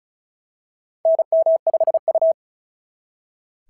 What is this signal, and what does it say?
Morse code, heard as a keyed tone, about 34 wpm, 660 Hz. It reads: NM5U
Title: SOTA QSO Practice 35wpm